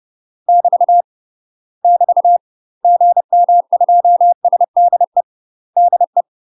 Morse code, heard as a keyed tone, about 30 wpm, 700 Hz. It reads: <BT> <BT> GM2SDE DE
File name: UK MASTER QSOs TRAINING / 30WPM / UK MASTER QSOs 0020